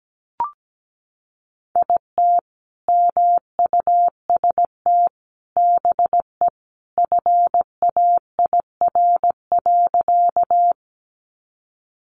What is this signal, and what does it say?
Morse code, heard as a keyed tone, about 17 wpm, 700 Hz. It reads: IT MUST BE FAIR.
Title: Sentences from Top 1300 Words 17wpm